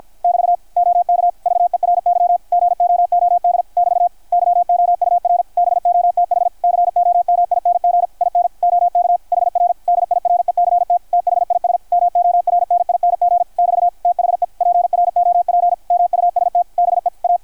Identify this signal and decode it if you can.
Morse code, heard as a keyed tone, about 53 wpm, 690 Hz, with radio noise. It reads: <BT> OK VERY GOOD <BT> YOUR BOTH COMING IN OK HR DIRECT THIS MORNING <BT> THE PROP MUST BE A
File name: archived / CRSnet